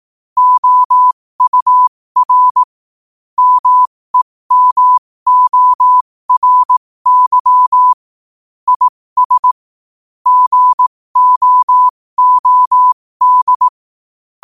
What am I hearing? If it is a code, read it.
Morse code, heard as a keyed tone, about 18 wpm, spaced as for 15 wpm, 1000 Hz. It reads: OUR MEMORY IS GOOD
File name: qotdfiles / 15wpm